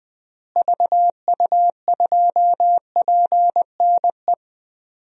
Morse code, heard as a keyed tone, about 20 wpm, 700 Hz. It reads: VU2PNE
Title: DXCC-100 Call Signs - 6 Characters - Repeated 2x - 20wpm